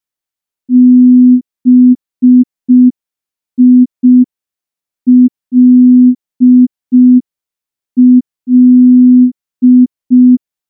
Morse code, heard as a keyed tone, about 5 wpm, 250 Hz. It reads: BILL
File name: morse / morse-training-05WPM